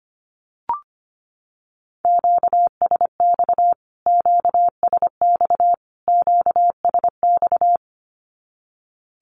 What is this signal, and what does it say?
Morse code, heard as a keyed tone, about 25 wpm, 700 Hz. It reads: QSX QSX QSX
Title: Q-Codes Repeated 3x 25wpm